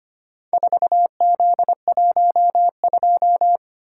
Morse code, heard as a keyed tone, about 25 wpm, 700 Hz. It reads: 4Z12
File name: practice / 25wpm / Four Letter-Number Combinations - 25wpm